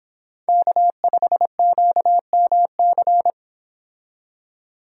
Morse code, heard as a keyed tone, about 26 wpm, 700 Hz. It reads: K5QMC